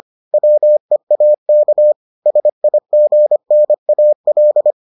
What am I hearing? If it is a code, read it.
Morse code, heard as a keyed tone, about 25 wpm, 600 Hz. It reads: WEAK SIGNAL